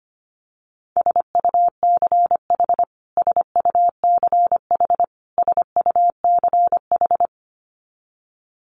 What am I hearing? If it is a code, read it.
Morse code, heard as a keyed tone, about 25 wpm, 700 Hz. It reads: SUCH SUCH SUCH
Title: Top 500 Words Repeated 25wpm